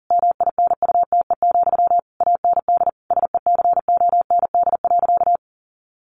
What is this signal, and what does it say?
Morse code, heard as a keyed tone, about 40 wpm, 700 Hz. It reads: MINUTE, AND SECOND.
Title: A Sound Of Thunder 40wpm